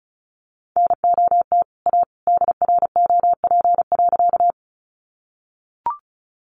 Morse code, heard as a keyed tone, about 34 wpm, 700 Hz. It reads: NOT A DROP.
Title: Sentences from Top 1400 Words 35wpm